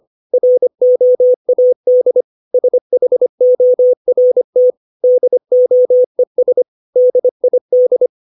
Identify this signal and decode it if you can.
Morse code, heard as a keyed tone, about 25 wpm, 500 Hz. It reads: ROAD SHORT DOES DID
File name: 632 words / 632 Words 25WPM